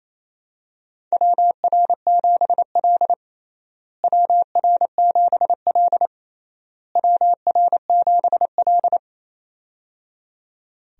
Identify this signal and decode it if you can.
Morse code, heard as a keyed tone, about 28 wpm, 700 Hz. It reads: WR7L WR7L WR7L
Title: USA Call Signs - 4 Characters - Repeated 3x - 28wpm